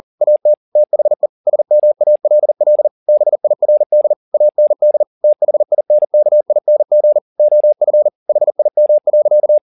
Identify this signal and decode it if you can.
Morse code, heard as a keyed tone, about 40 wpm, 600 Hz. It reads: AT THE SMALL BIRD AND THINKING OF HIM.